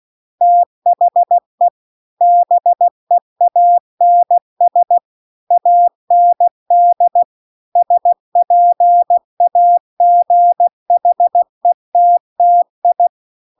Morse code, heard as a keyed tone, about 16 wpm, 700 Hz. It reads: THE BEANS AND SPAGHETTI